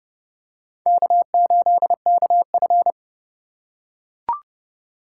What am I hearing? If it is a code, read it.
Morse code, heard as a keyed tone, about 29 wpm, 700 Hz. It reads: K8KF